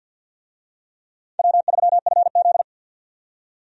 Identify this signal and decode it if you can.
Morse code, heard as a keyed tone, about 48 wpm, 700 Hz. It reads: W3FZ
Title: USA Call Signs - 4 Characters - 50wpm